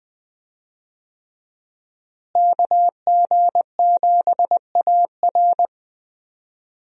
Morse code, heard as a keyed tone, about 20 wpm, 700 Hz. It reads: KG7AR
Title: USA Call Signs - 5 Characters - Repeated 3x - 20wpm